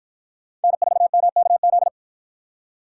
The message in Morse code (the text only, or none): N4MKZ